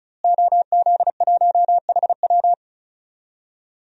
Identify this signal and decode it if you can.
Morse code, heard as a keyed tone, about 36 wpm, 700 Hz. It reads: OZ1HW
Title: DXCC-100 Call Signs - 5 Characters - Repeated 3x - 35wpm